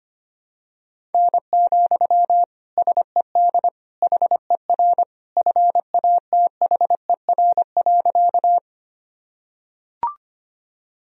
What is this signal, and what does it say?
Morse code, heard as a keyed tone, about 25 wpm, 700 Hz. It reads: N, SED HER FATHER.
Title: Sentences from Top 300 Words and Abbreviations 25wpm